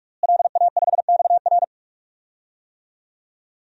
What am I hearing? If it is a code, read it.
Morse code, heard as a keyed tone, about 48 wpm, 700 Hz. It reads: LA5XR
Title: DXCC-100 Call Signs - 5 Characters - Repeated 3x - 45wpm